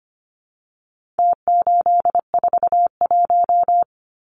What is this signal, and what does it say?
Morse code, heard as a keyed tone, about 25 wpm, 700 Hz. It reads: T841
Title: Numbers - Contest Style - 4-Digit 25wpm